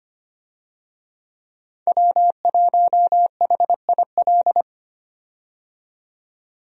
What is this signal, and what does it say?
Morse code, heard as a keyed tone, about 25 wpm, 700 Hz. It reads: W1HIL
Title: USA Call Signs - 5 Characters - 25wpm